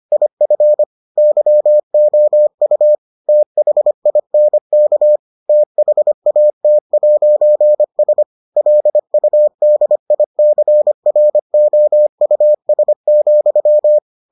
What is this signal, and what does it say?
Morse code, heard as a keyed tone, about 25 wpm, 600 Hz. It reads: IF YOU THINK THAT'S LUDICROUS,